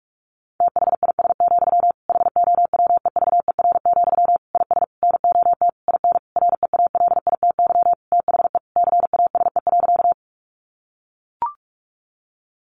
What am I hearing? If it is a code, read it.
Morse code, heard as a keyed tone, about 44 wpm, 700 Hz. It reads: THIS, HOWEVER, IS NOT IN REALITY THE CASE.
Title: Sentences from Top 1200 Words 45wpm